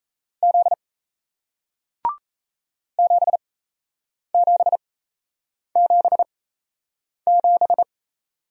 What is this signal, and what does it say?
Morse code, heard as a keyed tone, about 38 wpm, spaced as for 8 wpm, 700 Hz. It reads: Z 7777